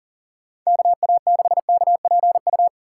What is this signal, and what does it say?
Morse code, heard as a keyed tone, about 40 wpm, 690 Hz. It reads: KA6KPU
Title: USA Call Signs - 6 Characters - 2x Character Spacing - 40wpm